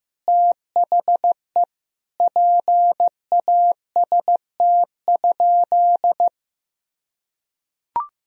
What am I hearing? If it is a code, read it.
Morse code, heard as a keyed tone, about 15 wpm, 700 Hz. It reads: THE PAST?
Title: Sentences from Top 1000 Words - Review Part 4 - 15wpm